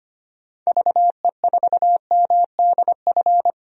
Morse code, heard as a keyed tone, about 25 wpm, 700 Hz. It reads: VE4MDF
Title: DXCC-100 Call Signs - 6 Characters - 25wpm